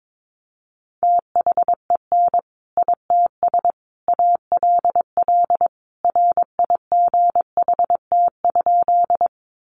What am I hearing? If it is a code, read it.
Morse code, heard as a keyed tone, about 22 wpm, 700 Hz. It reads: THEN ITS ALL RIGHT?
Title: Sentences from Top 600 Words 22wpm